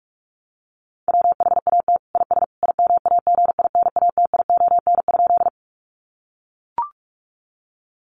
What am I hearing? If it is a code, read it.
Morse code, heard as a keyed tone, about 44 wpm, 700 Hz. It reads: WHAT IS IMAGINATION?
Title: Sentences from Top 1300 Words 45wpm